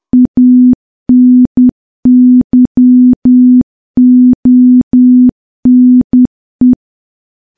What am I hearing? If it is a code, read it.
Morse code, heard as a keyed tone, about 10 wpm, 250 Hz. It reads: ANYONE
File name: morse / morse-training-10WPM